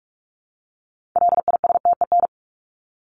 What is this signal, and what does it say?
Morse code, heard as a keyed tone, about 46 wpm, 700 Hz. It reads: LISTEN
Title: Top 500 Words Repeated 45wpm